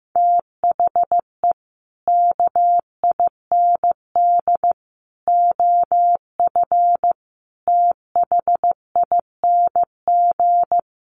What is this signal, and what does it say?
Morse code, heard as a keyed tone, about 15 wpm, 700 Hz. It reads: THE KIND OF THING
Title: Sets of 5 Words Encore 15wpm